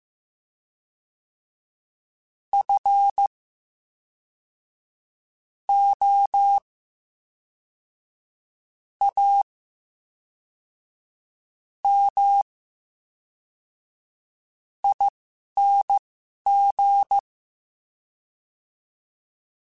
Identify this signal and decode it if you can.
Morse code, heard as a keyed tone, about 15 wpm, spaced as for 4 wpm, 800 Hz. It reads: FOAMING
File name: Special INGs / Special INGs-mp3s / VerySlowToFaster Ing @10